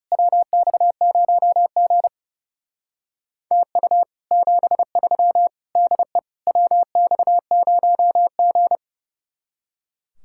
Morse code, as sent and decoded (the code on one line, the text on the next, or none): .-- -..- ----- --. / - ..- / --... ...-- / -.. / .-- -..- ----- --.
WX0G TU 73 DE WX0G